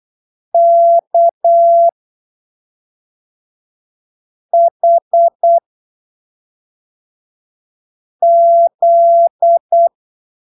Morse code, heard as a keyed tone, about 8 wpm, spaced as for 3 wpm, 680 Hz. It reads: KHZ
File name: arquivos / texto / texto 12 08ppm